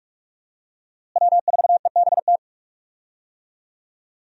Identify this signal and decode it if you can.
Morse code, heard as a keyed tone, about 46 wpm, 700 Hz. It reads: W4EBT
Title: USA Call Signs - 5 Characters - Repeated 3x - 45wpm